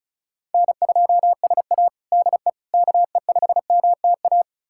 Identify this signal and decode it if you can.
Morse code, heard as a keyed tone, about 34 wpm, 700 Hz. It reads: N2SA DE KE5MTA
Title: USA Callsign DE Callsign 35wpm